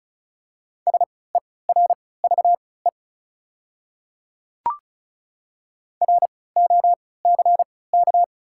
Morse code, heard as a keyed tone, about 34 wpm, spaced as for 20 wpm, 700 Hz. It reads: SERVE ROCK
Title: 3x Character Spacing - Top 500 Words - 35wpm